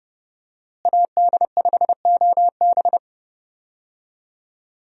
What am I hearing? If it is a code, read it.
Morse code, heard as a keyed tone, about 31 wpm, 700 Hz. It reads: AD5OB
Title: USA Call Signs - 5 Characters - Repeated 2x - 30wpm